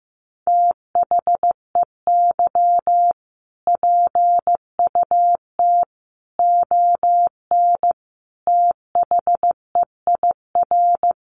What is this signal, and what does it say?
Morse code, heard as a keyed tone, about 15 wpm, 700 Hz. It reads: THEY PUT ON THEIR
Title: A Sound Of Thunder 15wpm